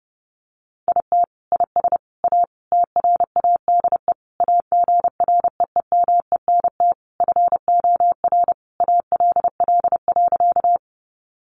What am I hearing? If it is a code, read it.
Morse code, heard as a keyed tone, about 30 wpm, 700 Hz. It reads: IT IS A TRADE AGREEMENT FOR ALL.
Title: Sentences from Top 1500 Words 30wpm